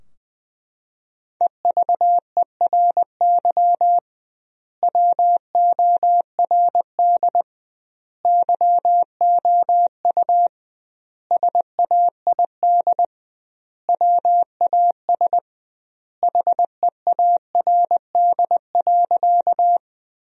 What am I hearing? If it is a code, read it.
Morse code, heard as a keyed tone, about 20 wpm, 700 Hz. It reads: EVERY WORD YOU SAID WAS HEARD.
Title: 2x Word-Spacing - Sentences from Top 400 Words - 20wpm